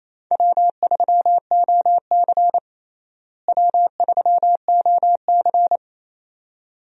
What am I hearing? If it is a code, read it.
Morse code, heard as a keyed tone, about 28 wpm, 700 Hz. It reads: W3OC W3OC
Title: USA Call Signs - 4 Characters - Repeated 2x - 28wpm